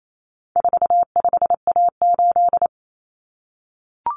Morse code, heard as a keyed tone, about 28 wpm, 700 Hz. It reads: 45A8
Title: Common Cut Numbers - 4-Digit 28wpm